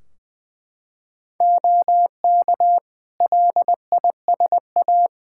Morse code, heard as a keyed tone, about 20 wpm, 700 Hz. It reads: OK LISA
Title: Practice QSO - Exchange 3 - More Info - 20wpm